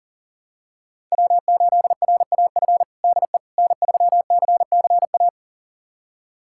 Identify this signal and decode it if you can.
Morse code, heard as a keyed tone, about 40 wpm, 700 Hz. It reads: W8RAF DE N3CCA
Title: USA Callsign DE Callsign - 5 Characters - 40wpm